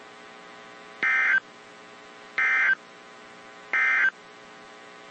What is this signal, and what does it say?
Emergency Alert System (SAME) tones, heard as harsh buzzing bursts. It is heard from 0.9 to 4.2 s.
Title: WZ2523 - NOAA Weather Radio